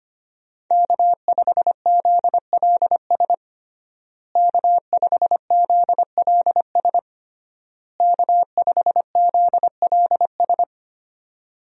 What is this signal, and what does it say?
Morse code, heard as a keyed tone, about 25 wpm, 700 Hz. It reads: K5ZLS K5ZLS K5ZLS
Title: USA Call Signs - 5 Characters - Repeated 3x - 25wpm